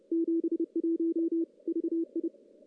Morse code, heard as a keyed tone, about 30 wpm, 330 Hz. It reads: 71 VI
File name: contests / CW-SS-k5tr- -1600